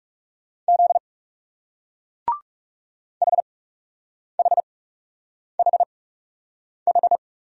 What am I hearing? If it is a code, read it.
Morse code, heard as a keyed tone, about 36 wpm, spaced as for 8 wpm, 700 Hz. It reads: Z HHHH
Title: Speed-Racing Single Letters 30wpm